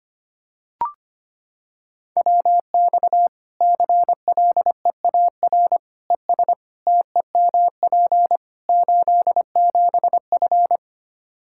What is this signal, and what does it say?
Morse code, heard as a keyed tone, about 25 wpm, 700 Hz. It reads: WX CLEAR ES TEMP 87F
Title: Practice QSO - Exchange 2 - Info - 25wpm